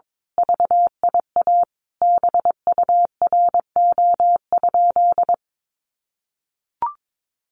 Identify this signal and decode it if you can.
Morse code, heard as a keyed tone, about 22 wpm, 700 Hz. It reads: VIA BURO?
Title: Common QSO Phrases 22wpm